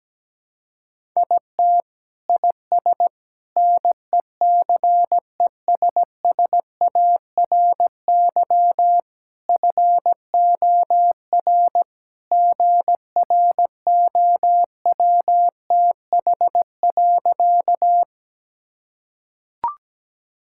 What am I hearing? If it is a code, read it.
Morse code, heard as a keyed tone, about 17 wpm, 700 Hz. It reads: IT IS NECESSARY FOR GROWTH.
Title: Sentences from Top 1000 Words - Review Part 5 - 17wpm